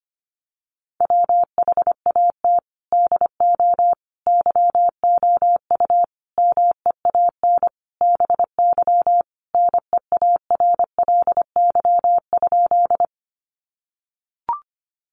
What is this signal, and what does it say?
Morse code, heard as a keyed tone, about 25 wpm, 700 Hz. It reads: WHAT DO YOU MEAN BY NEARLY?
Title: Sentences from Top 1300 Words 25wpm